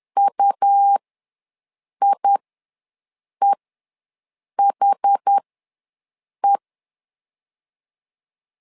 Morse code, heard as a keyed tone, about 10 wpm, spaced as for 6 wpm, 800 Hz. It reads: UIEHE
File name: LEZIONI / 2 UHDN / 2 UH ws 10wpm 2029